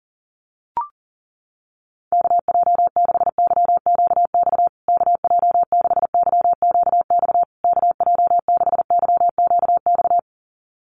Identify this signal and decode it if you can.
Morse code, heard as a keyed tone, about 40 wpm, 700 Hz. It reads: KJ6YQX KJ6YQX KJ6YQX
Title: USA Call Signs Repeated 3x 40wpm